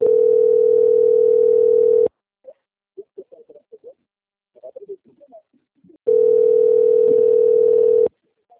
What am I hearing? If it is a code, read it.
Ringback tone (North America), 440 + 480 Hz.
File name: out- - - - - .13548